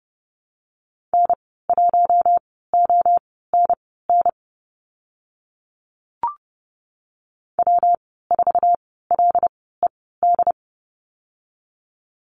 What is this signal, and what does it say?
Morse code, heard as a keyed tone, about 30 wpm, spaced as for 17 wpm, 700 Hz. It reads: N1ONN W4LED
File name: Lesson 19g - Letter w - Call Signs 30wpm